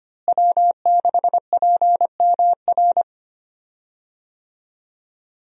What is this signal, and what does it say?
Morse code, heard as a keyed tone, about 25 wpm, 700 Hz. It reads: W6PMR
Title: USA Call Signs - 5 Characters - Repeated 3x - 25wpm